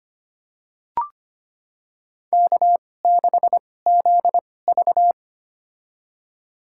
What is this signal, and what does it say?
Morse code, heard as a keyed tone, about 25 wpm, spaced as for 18 wpm, 700 Hz. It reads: K6ZV
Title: USA Call Signs - 4 Characters - 2x Character Spacing - 25wpm